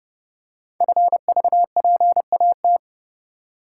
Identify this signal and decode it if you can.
Morse code, heard as a keyed tone, about 30 wpm, 700 Hz. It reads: FVPAT
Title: Code Groups - 5 Letters - 30wpm